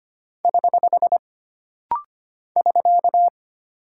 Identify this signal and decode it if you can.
Morse code, heard as a keyed tone, about 25 wpm, 700 Hz. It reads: <HH> <SK>
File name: Single Punctuation-Prosign - Rapid-Fire - 25wpm